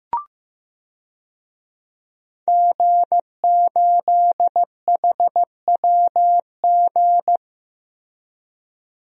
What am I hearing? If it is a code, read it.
Morse code, heard as a keyed tone, about 16 wpm, 700 Hz. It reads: G8HWG
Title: DXCC-100 Call Signs - 5 Characters - Repeated 3x - 15wpm